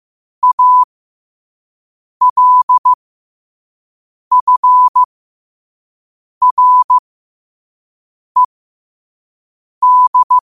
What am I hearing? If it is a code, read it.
Morse code, heard as a keyed tone, about 15 wpm, spaced as for 5 wpm, 1000 Hz. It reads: ALFRED
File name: qotdfiles / 5wpm